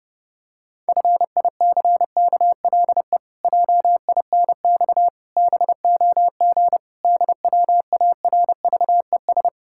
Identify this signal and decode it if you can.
Morse code, heard as a keyed tone, about 30 wpm, 700 Hz. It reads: FICKLE JINX BOG DWARVES
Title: Pangram Challenge v2 - 30wpm